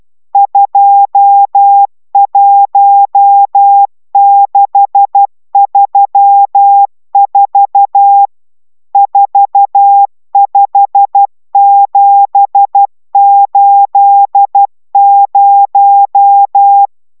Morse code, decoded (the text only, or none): 21634 45780